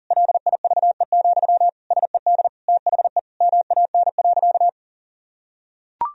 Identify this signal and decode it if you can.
Morse code, heard as a keyed tone, about 40 wpm, 700 Hz. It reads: LIVE, SED THE MAN.
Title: Sentences from Top 200 Words and Abbreviations 40wpm